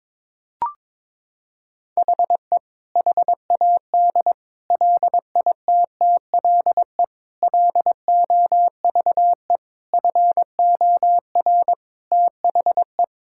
Morse code, heard as a keyed tone, about 22 wpm, 700 Hz. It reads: HE HAD LITTLE LOVE FOR THE
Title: Sentences from Top 1600 Words 22wpm